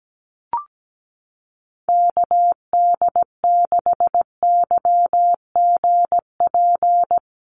Morse code, heard as a keyed tone, about 17 wpm, 700 Hz. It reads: KD6YGP
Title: USA Call Signs Repeated 3x 17wpm